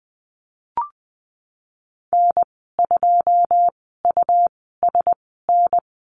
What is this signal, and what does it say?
Morse code, heard as a keyed tone, about 20 wpm, spaced as for 15 wpm, 700 Hz. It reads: N2USN
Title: 167 Lesson 27g - Number 7 - Call Signs 20wpm